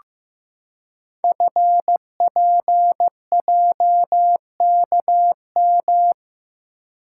Morse code, heard as a keyed tone, about 15 wpm, 700 Hz. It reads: FPJKM
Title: Code Groups - 5 Letter:Numbers - 15wpm